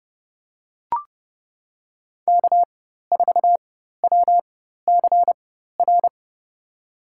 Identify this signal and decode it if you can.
Morse code, heard as a keyed tone, about 30 wpm, spaced as for 14 wpm, 700 Hz. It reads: K4WCR
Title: USA Call Signs - 5 Characters - 4x Character Spacing - 30wpm